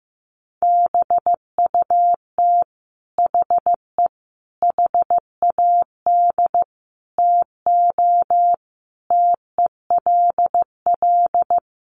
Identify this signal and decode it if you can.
Morse code, heard as a keyed tone, about 15 wpm, 700 Hz. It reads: BUT HE HAD TO TELL
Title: Sentences from Top 800 Words 15wpm